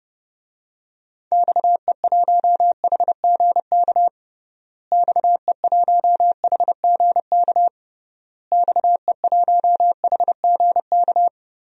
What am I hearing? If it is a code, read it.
Morse code, heard as a keyed tone, about 32 wpm, 700 Hz. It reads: XE1HGK XE1HGK XE1HGK
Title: DXCC-100 Call Signs - 6 Characters - Repeated 3x - 30wpm